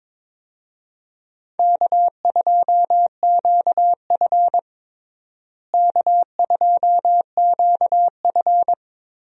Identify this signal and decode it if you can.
Morse code, heard as a keyed tone, about 22 wpm, 700 Hz. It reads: K2QF K2QF
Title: USA Call Signs - 4 Characters - Repeated 2x - 22wpm